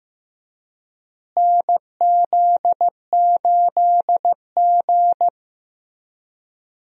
Morse code, heard as a keyed tone, about 15 wpm, 700 Hz. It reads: NZ8G
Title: USA Call Signs - 4 Characters - Repeated 2x - 15wpm